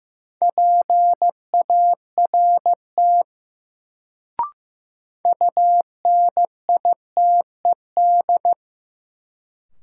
Morse code, heard as a keyed tone, about 15 wpm, 700 Hz. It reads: PART UNITED
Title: Sets of 2 Words - New Word Drill - 15wpm